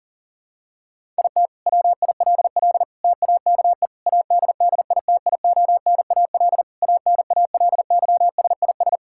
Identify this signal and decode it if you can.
Morse code, heard as a keyed tone, about 40 wpm, 690 Hz. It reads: IT WILL TAKE ADDITIONAL ANALYSIS
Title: Sentences from Top 1000 Words - Review Part 5 - 40wpm